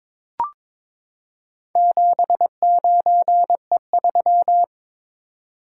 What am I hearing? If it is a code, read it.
Morse code, heard as a keyed tone, about 23 wpm, 700 Hz. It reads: 79E3
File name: practice / 22wpm / Four Letter-Number Combinations - 22wpm